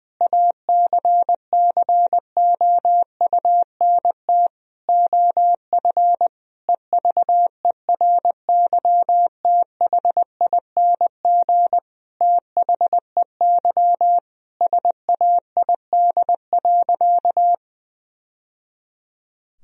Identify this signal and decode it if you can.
Morse code, heard as a keyed tone, about 20 wpm, 700 Hz. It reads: ACCOUNT OF EVERYTHING THEY SAID.
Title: Sentences from Top 1000 Words - Review Part 3 - 20wpm